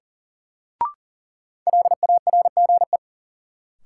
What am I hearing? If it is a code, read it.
Morse code, heard as a keyed tone, about 40 wpm, 700 Hz. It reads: LARGE
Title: Sets of 4 Words - New Word Drill - 40wpm